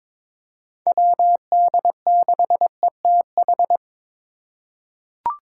Morse code, heard as a keyed tone, about 22 wpm, 700 Hz. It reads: WD6ETH
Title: USA Call Signs - 8x Character Spacing - 22wpm